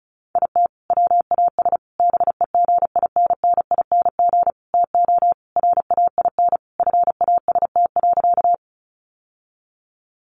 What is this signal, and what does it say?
Morse code, heard as a keyed tone, about 34 wpm, 700 Hz. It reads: IT WAS BEGINNING TO RAIN FAST.
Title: Sentences from Top 800 Words 35wpm